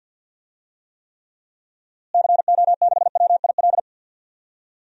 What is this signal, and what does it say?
Morse code, heard as a keyed tone, about 52 wpm, 700 Hz. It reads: CO6WIL
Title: DXCC-100 Call Signs - 6 Characters - Repeated 2x - 50wpm